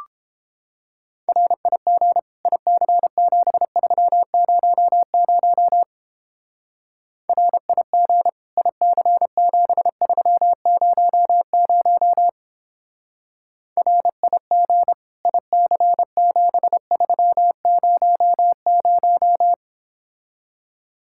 Morse code, heard as a keyed tone, about 29 wpm, 700 Hz. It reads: RIG IC7300 RIG IC7300 RIG IC7300